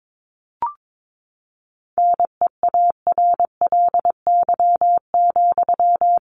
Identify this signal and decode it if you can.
Morse code, heard as a keyed tone, about 22 wpm, 700 Hz. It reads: NEARLY,
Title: Sentences from Top 1300 Words 22wpm